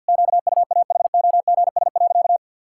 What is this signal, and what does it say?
Morse code, heard as a keyed tone, about 52 wpm, 700 Hz. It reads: QUAHOGS.
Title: Pangram Challenge v2 - 50wpm